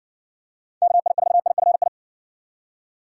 Morse code, heard as a keyed tone, about 53 wpm, 700 Hz. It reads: KI4IFI